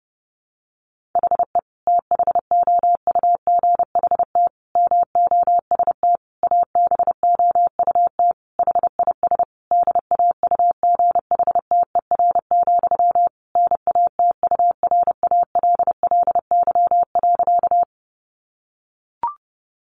Morse code, heard as a keyed tone, about 28 wpm, 700 Hz. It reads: HE THOUGHT MOST ABOUT HIS DAUGHTER, NATURALLY.
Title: Sentences from Top 1500 Words 30wpm